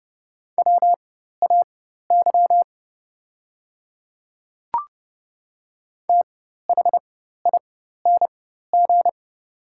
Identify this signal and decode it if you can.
Morse code, heard as a keyed tone, about 30 wpm, spaced as for 14 wpm, 700 Hz. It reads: WAY THING
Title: 4x Character Spacing - Top 100 Words - 30wpm